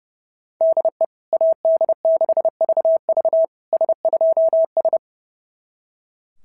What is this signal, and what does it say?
Morse code, heard as a keyed tone, about 28 wpm, 650 Hz. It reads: DE AD6VV S2S
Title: SOTA QSO Practice 30wpm